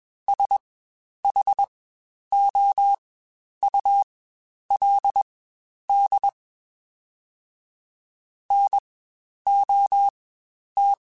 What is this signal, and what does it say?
Morse code, heard as a keyed tone, about 21 wpm, spaced as for 10 wpm, 800 Hz. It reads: SHOULD NOT
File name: Spaced / Chuckles Spaced-mp3s / Temptation sp @20